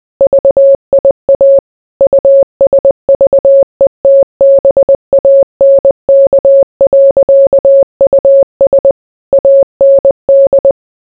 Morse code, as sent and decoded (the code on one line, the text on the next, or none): ...- .. .- / ..- ... ...- . - -... .- -. -.- .-.-.- ..- ... / .- -. -..
VIA USVETBANK.US AND